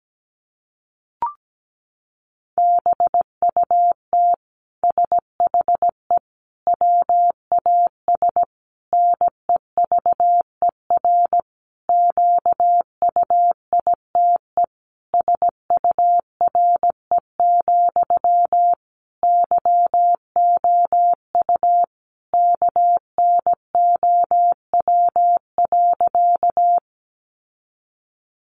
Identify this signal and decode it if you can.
Morse code, heard as a keyed tone, about 17 wpm, 700 Hz. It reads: BUT SHE WAS NEVER QUITE SURE, YOU KNOW.
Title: Sentences from Top 700 Words 17wpm